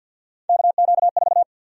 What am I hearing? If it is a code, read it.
Morse code, heard as a keyed tone, about 52 wpm, 700 Hz. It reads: KQ4
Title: USA Call Sign Prefixes - Repeated 2x - 50wpm